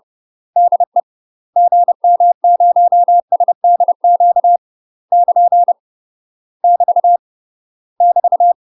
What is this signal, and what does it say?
Morse code, heard as a keyed tone, about 30 wpm, 700 Hz. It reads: DE GM0SDQ <KN> <BT> <BT>
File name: UK MASTER QSOs TRAINING / 30WPM / UK MASTER QSOs 0041